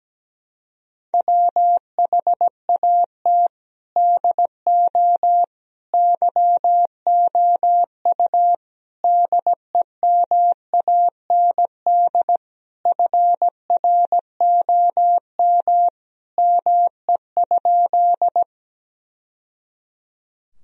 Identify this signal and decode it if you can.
Morse code, heard as a keyed tone, about 17 wpm, 700 Hz. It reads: WHAT DO YOU DEMAND FROM ME?
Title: Sentences from Top 1000 Words - Review Part 4 - 17wpm